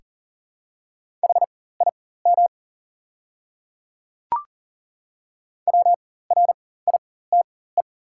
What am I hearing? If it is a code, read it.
Morse code, heard as a keyed tone, about 37 wpm, spaced as for 18 wpm, 690 Hz. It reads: HIM WRITE
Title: 4x Character Spacing - Top 100 Words - 40wpm